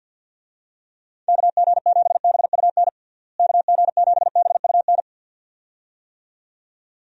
Morse code, heard as a keyed tone, about 48 wpm, 700 Hz. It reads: KG7BUN KG7BUN
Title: USA Call Signs - 6 Characters - Repeated 2x - 50wpm